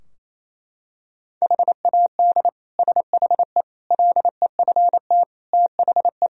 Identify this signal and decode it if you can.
Morse code, heard as a keyed tone, about 28 wpm, 700 Hz. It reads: HAD SHE LEFT THE
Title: Sentences from Top 1000 Words - Review Part 3 - 28wpm